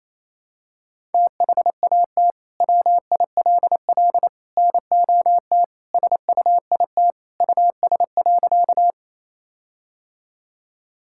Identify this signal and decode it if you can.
Morse code, heard as a keyed tone, about 28 wpm, 700 Hz. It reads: THAT WILL NOT SUIT US.
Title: Sentences from Top 1600 Words 28wpm